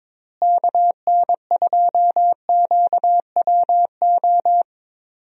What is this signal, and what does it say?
Morse code, heard as a keyed tone, about 22 wpm, 700 Hz. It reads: KN2QWO
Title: USA Call Signs - 6 Characters - Repeated 2x - 22wpm